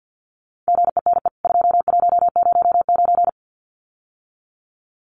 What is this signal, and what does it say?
Morse code, heard as a keyed tone, about 50 wpm, 700 Hz. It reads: GENE 2109
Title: CWT Exchange 50wpm